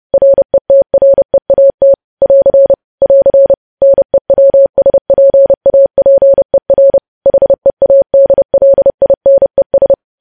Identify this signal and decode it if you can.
Morse code, heard as a keyed tone, about 30 wpm, 550 Hz. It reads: RETREAT <AR> <AR> NEWSPAPER HEADLINES
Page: news headlines in morse code at 30 wpm